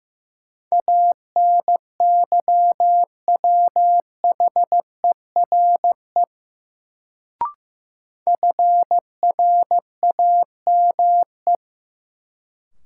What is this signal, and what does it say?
Morse code, heard as a keyed tone, about 15 wpm, 700 Hz. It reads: ANYWHERE FRAME
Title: Sentences from Top 1400 Words - New Word Drill Alt - 15wpm